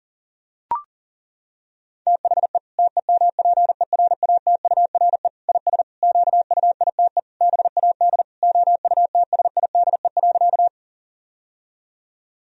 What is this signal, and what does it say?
Morse code, heard as a keyed tone, about 40 wpm, 700 Hz. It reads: THE TEMPERATURE IS QUITE BAD OUTSIDE.
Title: Sentences from Top 1000 Words - Review Part 4 - 40wpm